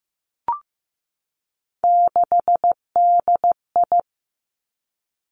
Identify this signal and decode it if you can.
Morse code, heard as a keyed tone, about 15 wpm, 700 Hz. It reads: 6DI
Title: Three Challenging Letter-Number Combinations 15wpm